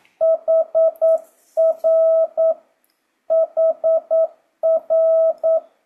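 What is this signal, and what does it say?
Morse code, heard as a keyed tone, about 9 wpm, 650 Hz. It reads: HR HR
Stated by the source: Morse code speed-up practice at 9 WPM, from JH8CBH.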